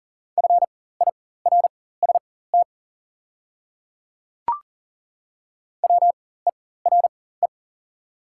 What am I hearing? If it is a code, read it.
Morse code, heard as a keyed tone, about 38 wpm, spaced as for 18 wpm, 700 Hz. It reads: FIRST WERE
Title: 4x Character Spacing - Top 500 Words - 40wpm